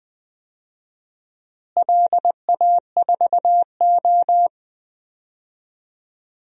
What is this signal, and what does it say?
Morse code, heard as a keyed tone, about 20 wpm, 700 Hz. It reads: LA4O
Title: DXCC-100 Call Signs - 4 Characters - Repeated 3x - 20wpm